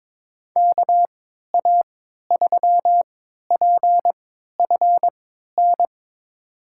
Morse code, heard as a keyed tone, about 22 wpm, spaced as for 13 wpm, 700 Hz. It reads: KA3PFN
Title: USA Call Signs - 6 Characters - 3x Character Spacing - 22wpm